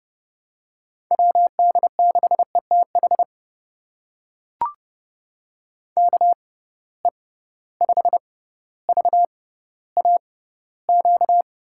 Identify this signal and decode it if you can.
Morse code, heard as a keyed tone, about 30 wpm, spaced as for 15 wpm, 700 Hz. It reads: WD6ETH KE5VAQ
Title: USA Call Signs - 6 Characters - 6x Character Spacing - 30wpm